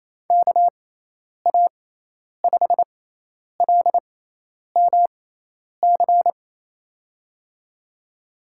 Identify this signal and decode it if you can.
Morse code, heard as a keyed tone, about 28 wpm, spaced as for 10 wpm, 700 Hz. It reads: KA5LMC
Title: USA Call Signs - 6 Characters - 6x Character Spacing - 28wpm